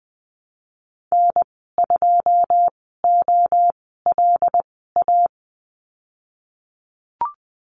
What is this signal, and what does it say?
Morse code, heard as a keyed tone, about 20 wpm, spaced as for 15 wpm, 700 Hz. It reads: N2OLA
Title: 186 Lesson 30g - Letter b - Call Signs 20wpm